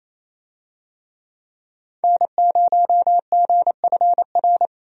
Morse code, heard as a keyed tone, about 28 wpm, 700 Hz. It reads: N0GFR
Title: USA Call Signs - 5 Characters - 28wpm